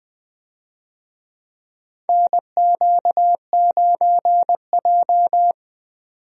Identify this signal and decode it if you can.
Morse code, heard as a keyed tone, about 20 wpm, 700 Hz. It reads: NQ9J
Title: USA Call Signs - 4 Characters - 20wpm